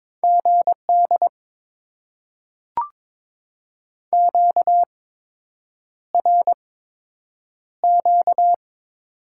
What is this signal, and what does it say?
Morse code, heard as a keyed tone, about 22 wpm, spaced as for 8 wpm, 700 Hz. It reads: GD QRQ
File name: Top 90 QSO Elements - 8x Character Spacing - 22wpm